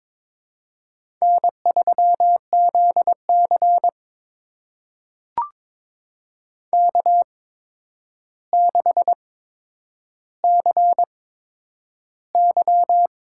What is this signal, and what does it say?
Morse code, heard as a keyed tone, about 22 wpm, spaced as for 9 wpm, 700 Hz. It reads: N3ZC K6CY
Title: USA Call Signs - 4 Characters - 8x Character Spacing - 22wpm